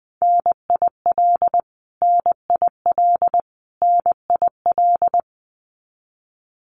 Morse code, heard as a keyed tone, about 20 wpm, 700 Hz. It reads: NIL NIL NIL
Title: Abbreviations Repeated 3x - 20wpm